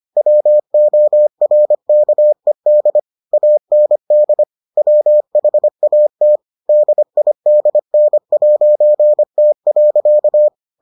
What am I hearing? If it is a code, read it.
Morse code, heard as a keyed tone, about 25 wpm, 600 Hz. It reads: WORKED AND WHAT DIDN'T.